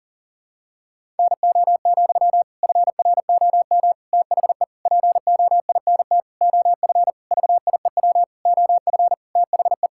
Morse code, heard as a keyed tone, about 40 wpm, 700 Hz. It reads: NO, FROM THE POINT OF VIEW OF THE
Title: Sentences from Top 1000 Words - Review Part 6 - 40wpm